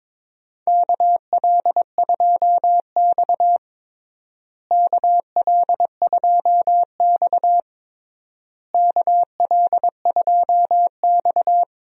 Morse code, heard as a keyed tone, about 22 wpm, 700 Hz. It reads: KL2X KL2X KL2X
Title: USA Call Signs - 4 Characters - Repeated 3x - 22wpm